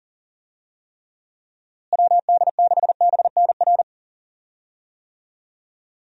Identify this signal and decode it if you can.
Morse code, heard as a keyed tone, about 40 wpm, 700 Hz. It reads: WD6BNR
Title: USA Call Signs - 6 Characters - Repeated 2x - 40wpm